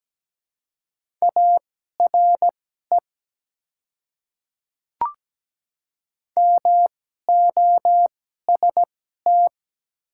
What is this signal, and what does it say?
Morse code, heard as a keyed tone, about 17 wpm, spaced as for 12 wpm, 700 Hz. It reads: ARE MOST
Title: 2x Character Spacing - Top 100 Words - 17wpm